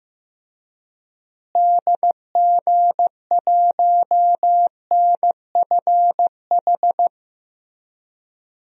Morse code, heard as a keyed tone, about 15 wpm, 700 Hz. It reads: DG1NFH
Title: DXCC-100 Call Signs - 6 Characters - Repeated 3x - 15wpm